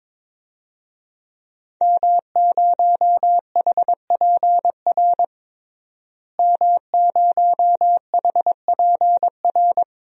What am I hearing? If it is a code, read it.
Morse code, heard as a keyed tone, about 22 wpm, 700 Hz. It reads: M0HPR M0HPR
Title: DXCC-100 Call Signs - 5 Characters - Repeated 2x - 22wpm